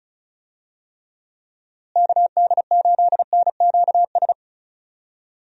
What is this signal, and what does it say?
Morse code, heard as a keyed tone, about 36 wpm, 700 Hz. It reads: KD8NQS
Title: USA Call Signs - 6 Characters - Repeated 3x - 35wpm